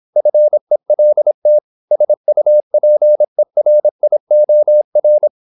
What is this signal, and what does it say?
Morse code, heard as a keyed tone, about 26 wpm, 600 Hz. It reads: FELT SUPERIOR
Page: Never 26wpm | CW med Gnister